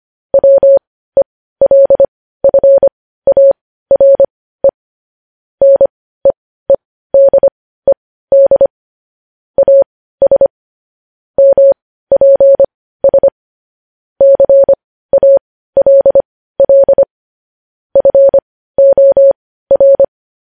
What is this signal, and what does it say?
Morse code, heard as a keyed tone, about 25 wpm, spaced as for 15 wpm, 550 Hz. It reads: WELFARE NEEDED AS MPS CALL FOR
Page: news headlines in morse code at 15 wpm